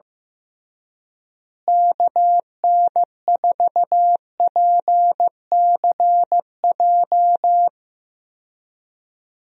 Morse code, heard as a keyed tone, about 15 wpm, 700 Hz. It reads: KN4PCJ
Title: USA Call Signs - 6 Characters - Repeated 3x - 15wpm